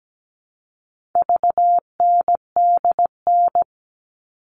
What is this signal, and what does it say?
Morse code, heard as a keyed tone, about 17 wpm, 700 Hz. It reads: VNDN
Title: All Cut Numbers - 4-Digit 17wpm